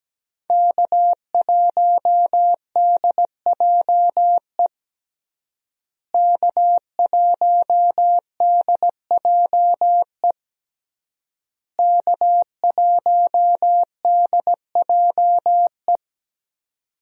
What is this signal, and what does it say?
Morse code, heard as a keyed tone, about 17 wpm, 700 Hz. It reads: K1DJE K1DJE K1DJE
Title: USA Call Signs - 5 Characters - Repeated 3x - 17wpm